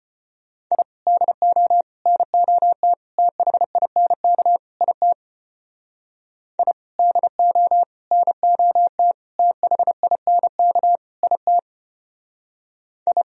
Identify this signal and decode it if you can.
Morse code, heard as a keyed tone, about 32 wpm, 700 Hz. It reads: I DO NOT THINK IT I DO NOT THINK IT I